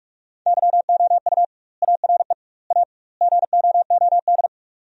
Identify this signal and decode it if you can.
Morse code, heard as a keyed tone, about 41 wpm, 700 Hz. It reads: YOU ARE A GOOD
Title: 2x Word-Spacing - Sentences from Top 200 Words - 45wpm